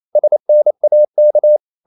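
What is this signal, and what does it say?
Morse code, heard as a keyed tone, about 28 wpm, 600 Hz. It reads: SNAK
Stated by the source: Danish short words sent in Morse at 28 wpm, from CW med Gnister.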